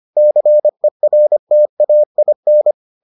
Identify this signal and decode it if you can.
Morse code, heard as a keyed tone, about 25 wpm, 600 Hz. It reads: CERTAIN